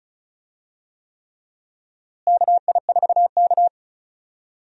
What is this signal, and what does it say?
Morse code, heard as a keyed tone, about 34 wpm, 700 Hz. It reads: KI4K
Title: USA Call Signs - 4 Characters - Repeated 3x - 35wpm